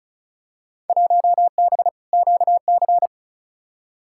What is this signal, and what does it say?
Morse code, heard as a keyed tone, about 34 wpm, 700 Hz. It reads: 1B QC
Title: Field Day Exchanges - 35wpm